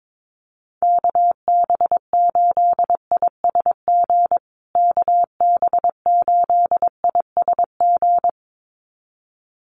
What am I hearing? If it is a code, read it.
Morse code, heard as a keyed tone, about 22 wpm, 700 Hz. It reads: KB8ISG KB8ISG
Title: USA Call Signs Repeated 2x 22wpm